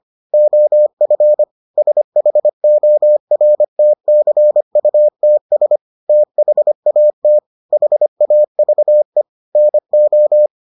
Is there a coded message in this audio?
Yes: Morse code from 0.3 to 10.5 s, heard as a keyed tone, about 25 wpm, 600 Hz. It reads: OF SHORTCUTS THAT HAVE NO